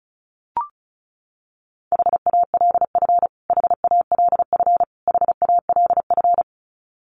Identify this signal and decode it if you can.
Morse code, heard as a keyed tone, about 34 wpm, 700 Hz. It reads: HALF HALF HALF
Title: Top 500 Words Repeated 35wpm